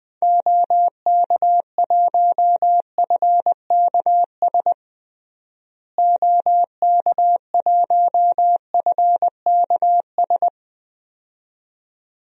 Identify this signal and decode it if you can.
Morse code, heard as a keyed tone, about 21 wpm, 700 Hz. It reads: OK1FKS OK1FKS
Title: DXCC-100 Call Signs - 6 Characters - Repeated 2x - 20wpm